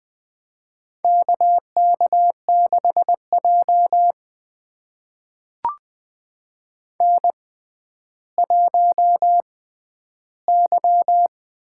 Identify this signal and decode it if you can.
Morse code, heard as a keyed tone, about 20 wpm, spaced as for 11 wpm, 700 Hz. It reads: KK6J N1Y
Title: USA Call Signs - 4 Characters - 6x Character Spacing - 20wpm